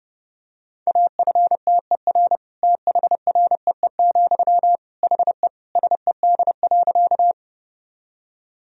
Morse code, heard as a keyed tone, about 32 wpm, 700 Hz. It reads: AFTER THREE, HE SED.
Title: Sentences from Top 200 Words and Abbreviations 30wpm